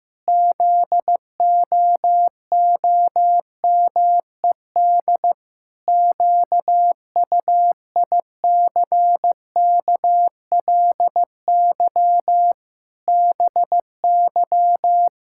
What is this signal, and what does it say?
Morse code, heard as a keyed tone, about 15 wpm, 700 Hz. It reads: ZOOMED QUICKLY BY
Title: Pangram Challenge v2 - 15wpm